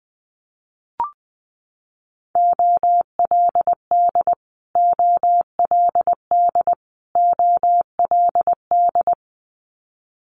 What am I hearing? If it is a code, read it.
Morse code, heard as a keyed tone, about 20 wpm, 700 Hz. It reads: OLD OLD OLD
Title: Multi-letter Phonemes Repeated 3x 20wpm